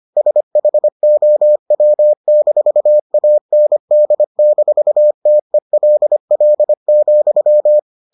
Morse code, heard as a keyed tone, about 25 wpm, 600 Hz. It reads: SHOW-AND-TELL,